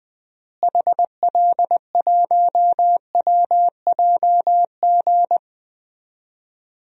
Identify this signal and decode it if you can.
Morse code, heard as a keyed tone, about 20 wpm, 700 Hz. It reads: HL1WJG
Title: DXCC-100 Call Signs - 6 Characters - Repeated 2x - 20wpm